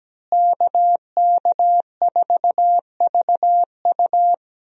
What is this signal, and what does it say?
Morse code, heard as a keyed tone, about 17 wpm, 700 Hz. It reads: KK4VU
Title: USA Call Signs - 5 Characters - Repeated 2x - 17wpm